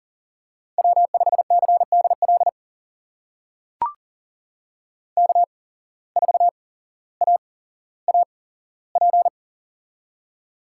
Morse code, heard as a keyed tone, about 39 wpm, spaced as for 17 wpm, 700 Hz. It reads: W5CDL K4AAP
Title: USA Call Signs - 5 Characters - 8x Character Spacing - 40wpm